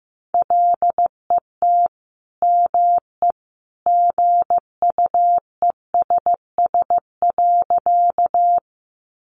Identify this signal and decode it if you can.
Morse code, heard as a keyed tone, about 15 wpm, 700 Hz. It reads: LET ME GUESS.
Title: Sentences from Top 1500 Words 15wpm